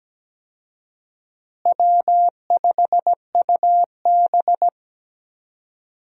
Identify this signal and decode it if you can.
Morse code, heard as a keyed tone, about 17 wpm, 700 Hz. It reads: W5UB
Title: USA Call Signs - 4 Characters - Repeated 2x - 17wpm